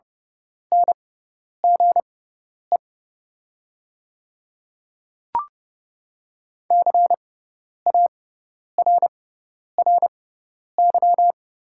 Morse code, heard as a keyed tone, about 30 wpm, spaced as for 10 wpm, 700 Hz. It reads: NGE CARRY
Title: 6x Character Spacing - Top 300 Words - 30wpm